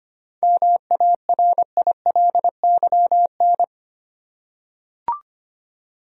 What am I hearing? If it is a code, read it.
Morse code, heard as a keyed tone, about 25 wpm, 700 Hz. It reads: MARILYN
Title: USA Common Names - 25wpm